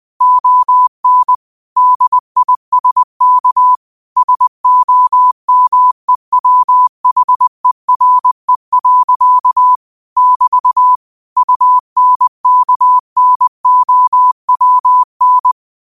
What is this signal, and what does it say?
Morse code, heard as a keyed tone, about 20 wpm, 1000 Hz. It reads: ON DISK SOMEWHERE. <BT> UNKNOWN
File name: qotdfiles / 20wpm